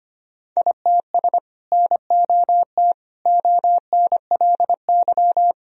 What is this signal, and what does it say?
Morse code, heard as a keyed tone, about 25 wpm, 700 Hz. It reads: ITS NOT ONLY
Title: Sentences from Top 1000 Words - Review Part 2 - 25wpm